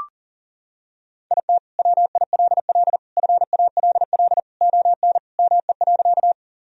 Morse code, heard as a keyed tone, about 40 wpm, 700 Hz. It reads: IT WILL FALL ON ME.
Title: Sentences from Top 1000 Words - Review Part 2 - 40wpm